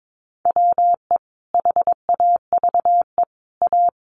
Morse code, heard as a keyed tone, about 22 wpm, 700 Hz. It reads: WE HAVE A
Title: Sentences from Top 1500 Words 22wpm